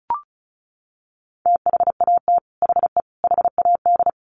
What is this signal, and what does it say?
Morse code, heard as a keyed tone, about 34 wpm, 700 Hz. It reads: THAT HE HAD
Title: Sets of 3 Words 35wpm